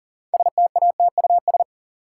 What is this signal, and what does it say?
Morse code, heard as a keyed tone, about 40 wpm, 700 Hz. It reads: STATUS
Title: Sentences from Top 1400 Words - New Word Drill Alt - 40wpm